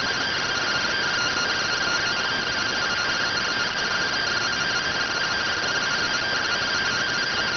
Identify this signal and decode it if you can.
Two-tone teleprinter signal, about 170 Hz shift.